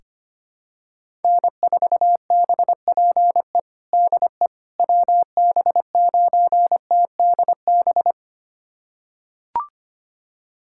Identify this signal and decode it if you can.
Morse code, heard as a keyed tone, about 25 wpm, 700 Hz. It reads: N4BPE DE WB9TDB
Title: USA Callsign DE Callsign 25wpm